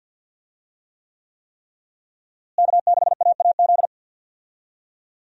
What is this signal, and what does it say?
Morse code, heard as a keyed tone, about 50 wpm, 700 Hz. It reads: K6AAZ
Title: USA Call Signs - 5 Characters - Repeated 3x - 50wpm